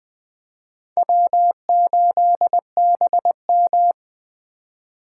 Morse code, heard as a keyed tone, about 23 wpm, 700 Hz. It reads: W8BM
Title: Speed Racing - USA 1x2 Call Signs 20wpm